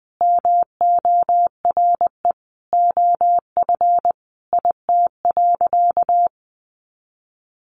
Morse code, heard as a keyed tone, about 20 wpm, 700 Hz. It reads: MORE OF IT.
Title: Sentences from Top 1300 Words 20wpm